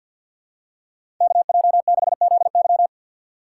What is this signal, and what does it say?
Morse code, heard as a keyed tone, about 50 wpm, 700 Hz. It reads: KJ6ZY
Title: USA Call Signs - 5 Characters - 2x Character Spacing - 50wpm